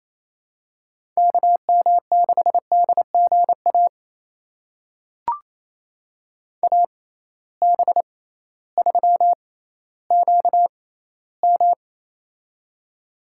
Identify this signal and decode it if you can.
Morse code, heard as a keyed tone, about 28 wpm, spaced as for 15 wpm, 700 Hz. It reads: KM6DGA AB3QM
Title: USA Call Signs - 6x Character Spacing - 28wpm